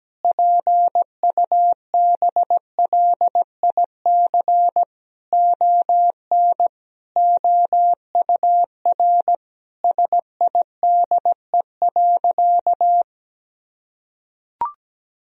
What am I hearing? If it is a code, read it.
Morse code, heard as a keyed tone, about 17 wpm, 700 Hz. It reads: PUBLIC ON OUR SIDE.
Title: Sentences from Top 1000 Words - Review Part 3 - 17wpm